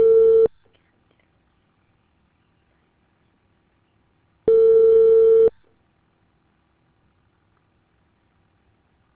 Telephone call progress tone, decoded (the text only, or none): ringback tone